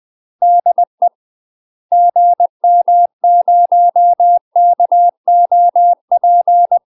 Morse code, heard as a keyed tone, about 20 wpm, 700 Hz. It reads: DE GM0KOP
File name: UK MASTER QSOs TRAINING / 20WPM / UK MASTER QSOs 0062